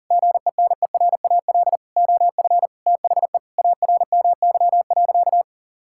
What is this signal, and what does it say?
Morse code, heard as a keyed tone, about 40 wpm, 700 Hz. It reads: GENERAL OF THE ARMY.
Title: Sentences from Top 1600 Words 40wpm